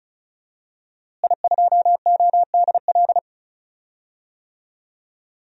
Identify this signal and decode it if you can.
Morse code, heard as a keyed tone, about 34 wpm, 700 Hz. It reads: I2ODL
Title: DXCC-100 Call Signs - 5 Characters - Repeated 3x - 35wpm